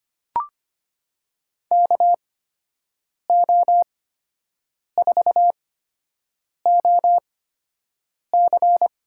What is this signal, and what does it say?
Morse code, heard as a keyed tone, about 25 wpm, spaced as for 7 wpm, 700 Hz. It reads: KO4OC
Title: USA Call Signs - 5 Characters - 8x Character Spacing - 25wpm